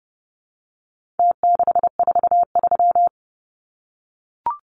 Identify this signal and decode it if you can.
Morse code, heard as a keyed tone, about 30 wpm, 700 Hz. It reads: T643
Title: Numbers - Contest Style - 4-Digit 30wpm